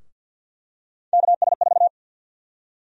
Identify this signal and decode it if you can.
Morse code, heard as a keyed tone, about 51 wpm, 700 Hz. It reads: KS4
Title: USA Call Sign Prefixes - Repeated 2x - 50wpm